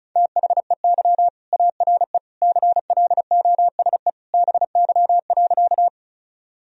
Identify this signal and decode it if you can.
Morse code, heard as a keyed tone, about 34 wpm, 700 Hz. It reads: THEY ARE CLOSE BY.
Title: Sentences from Top 1000 Words - Review Part 1 - 35wpm